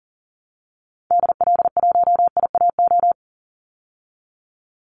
Morse code, heard as a keyed tone, about 40 wpm, 700 Hz. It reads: DL1IAO
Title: Call Signs Repeated 2x - 40wpm